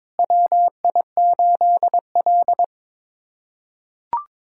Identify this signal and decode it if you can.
Morse code, heard as a keyed tone, about 21 wpm, 700 Hz. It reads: WI8L
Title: USA Callsign DE Callsign - 4 Characters - 22wpm